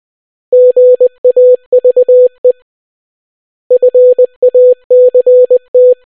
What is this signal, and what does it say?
Morse code, heard as a keyed tone, about 20 wpm, 500 Hz. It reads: GAVE FACT